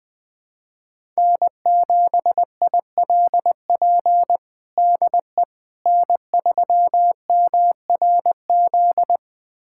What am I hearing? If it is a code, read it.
Morse code, heard as a keyed tone, about 21 wpm, 700 Hz. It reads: N7ILP DE N3MRZ
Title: USA Callsign DE Callsign - 5 Characters - 20wpm